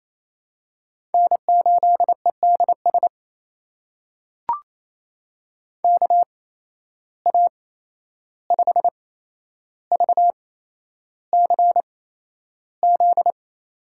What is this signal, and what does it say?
Morse code, heard as a keyed tone, about 28 wpm, spaced as for 11 wpm, 700 Hz. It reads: N8EDS KA5VCZ
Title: USA Call Signs - 8x Character Spacing - 28wpm